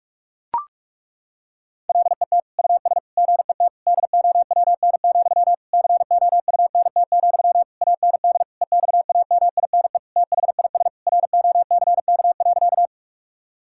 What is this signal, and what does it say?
Morse code, heard as a keyed tone, about 41 wpm, 700 Hz. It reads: LET US GET DOWN, COUNT, AND EXAMINE THIS ROCK.
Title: Sentences from Top 1600 Words 45wpm